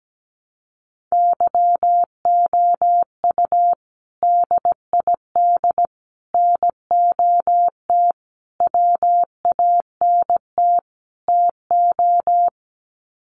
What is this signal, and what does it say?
Morse code, heard as a keyed tone, about 17 wpm, 700 Hz. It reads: YOU DID NOT WANT TO
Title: Sets of 5 Words Encore2 17wpm